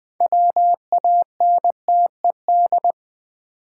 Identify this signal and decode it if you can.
Morse code, heard as a keyed tone, about 20 wpm, 700 Hz. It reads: WANTED
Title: Sets of 2 Words - New Word Drill - 20wpm